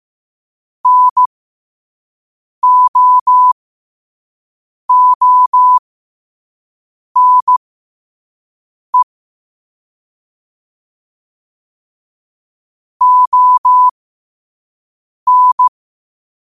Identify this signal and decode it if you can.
Morse code, heard as a keyed tone, about 15 wpm, spaced as for 5 wpm, 1000 Hz. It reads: NO ONE ON
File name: qotdfiles / 5wpm